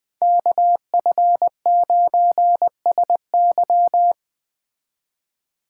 Morse code, heard as a keyed tone, about 20 wpm, 700 Hz. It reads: KF9SY
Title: Speed Racing - USA 2x2 Call Signs 17wpm